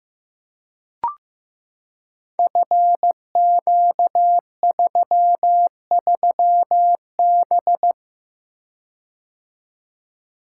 Morse code, heard as a keyed tone, about 15 wpm, 700 Hz. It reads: FQ33B
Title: Code Groups - 5 Letter:Numbers - 15wpm